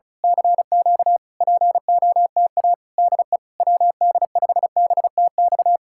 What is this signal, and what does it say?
Morse code, heard as a keyed tone, about 33 wpm, 700 Hz. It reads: CQ POTA DE WD5BTX
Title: Speed-Racing Common QSO Phrases Encore - 35wpm